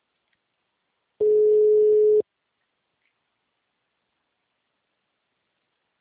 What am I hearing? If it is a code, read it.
Ringback tone (Europe, 425 Hz).
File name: out- -2041- - - .27179